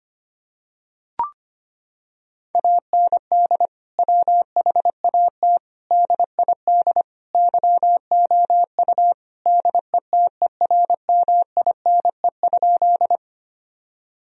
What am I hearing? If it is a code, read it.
Morse code, heard as a keyed tone, about 25 wpm, 700 Hz. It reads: AND WHAT DID YOU DETERMINE?
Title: Sentences from Top 1000 Words - Review Part 4 - 25wpm